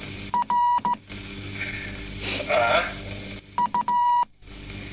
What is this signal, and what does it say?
Morse code, heard as a keyed tone, about 16 wpm, 970 Hz, with speech. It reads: R U